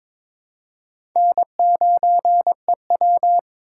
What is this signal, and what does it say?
Morse code, heard as a keyed tone, about 22 wpm, 700 Hz. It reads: N9EW
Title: USA Call Signs - 4 Characters - 3x Character Spacing - 22wpm